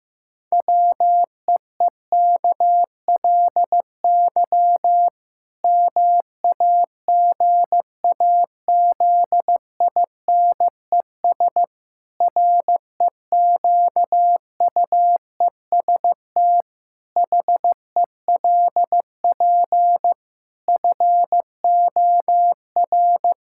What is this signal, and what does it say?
Morse code, heard as a keyed tone, about 15 wpm, 700 Hz. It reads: WEEKLY MAGAZINES REQUEST HELP FOR
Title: Pangram Challenge v2 - 15wpm